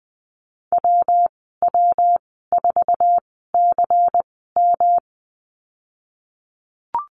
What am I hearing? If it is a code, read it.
Morse code, heard as a keyed tone, about 20 wpm, spaced as for 15 wpm, 700 Hz. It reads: WW4CM
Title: 161 Lesson 26g - Letter g - Call Signs 20wpm